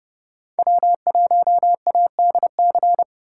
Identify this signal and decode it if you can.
Morse code, heard as a keyed tone, about 29 wpm, 700 Hz. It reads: W1ADC